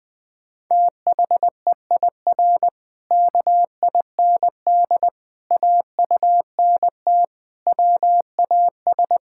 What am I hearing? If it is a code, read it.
Morse code, heard as a keyed tone, about 20 wpm, 700 Hz. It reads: THEIR KIND AUNT WAS
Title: Pangram Challenge v2 - 20wpm